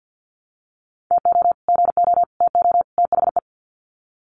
Morse code, heard as a keyed tone, about 50 wpm, 700 Hz. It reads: TO GO TO THE
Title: Sets of 4 Words 50wpm